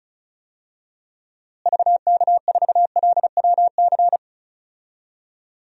Morse code, heard as a keyed tone, about 34 wpm, 700 Hz. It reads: VK4LWC
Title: DXCC-100 Call Signs - 6 Characters - Repeated 3x - 35wpm